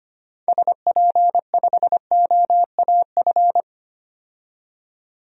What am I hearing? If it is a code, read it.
Morse code, heard as a keyed tone, about 25 wpm, 700 Hz. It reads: SP5OAF
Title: DXCC-100 Call Signs - 6 Characters - 25wpm